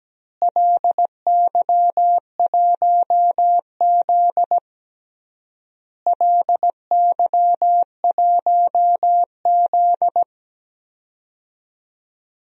Morse code, heard as a keyed tone, about 17 wpm, 700 Hz. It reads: LY1Z LY1Z
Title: DXCC-100 Call Signs - 4 Characters - Repeated 3x - 17wpm